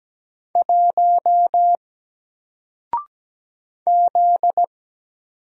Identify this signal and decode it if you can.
Morse code, heard as a keyed tone, about 17 wpm, 700 Hz. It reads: 1 Z
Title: Single Letter-Number 17wpm